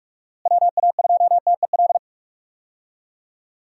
Morse code, heard as a keyed tone, about 47 wpm, 700 Hz. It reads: WA2TEL
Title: USA Call Signs - 6 Characters - Repeated 2x - 45wpm